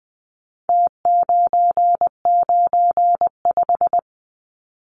Morse code, heard as a keyed tone, about 20 wpm, 700 Hz. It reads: T995
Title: Numbers - Contest Style - 4-Digit 20wpm